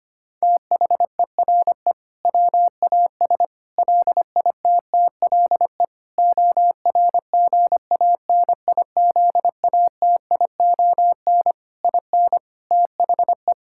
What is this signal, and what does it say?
Morse code, heard as a keyed tone, about 25 wpm, 700 Hz. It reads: THERE WAS LITTLE ORGANIZATION IN THE
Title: Sentences from Top 1000 Words - Review Part 5 - 25wpm